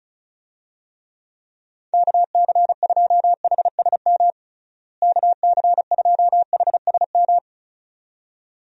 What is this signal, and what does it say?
Morse code, heard as a keyed tone, about 34 wpm, 700 Hz. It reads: KC2HSM KC2HSM
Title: USA Call Signs - 6 Characters - Repeated 2x - 35wpm